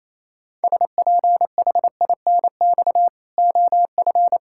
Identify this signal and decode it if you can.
Morse code, heard as a keyed tone, about 28 wpm, 700 Hz. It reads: SPHINX OF
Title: Pangram Challenge v2 - 28wpm